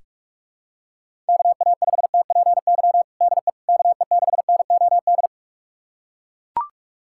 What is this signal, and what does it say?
Morse code, heard as a keyed tone, about 41 wpm, 700 Hz. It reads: KA5TPY DE KE6NOD
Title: USA Callsign DE Callsign - 6 Characters - 45wpm